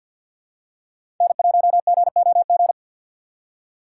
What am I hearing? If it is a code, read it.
Morse code, heard as a keyed tone, about 52 wpm, 700 Hz. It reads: N1GOG
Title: USA Call Signs - 5 Characters - 6x Character Spacing - 50wpm